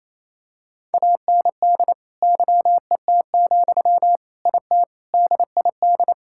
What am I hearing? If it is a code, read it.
Morse code, heard as a keyed tone, about 28 wpm, 700 Hz. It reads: AND YET, IT DID
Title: Sentences from Top 1600 Words 28wpm